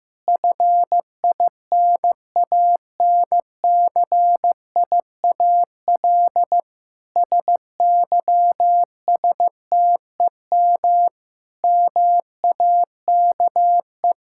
Morse code, heard as a keyed tone, about 15 wpm, 700 Hz. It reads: FINANCIAL SYSTEM MAKE
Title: Sentences from Top 1000 Words - Review Part 5 - 15wpm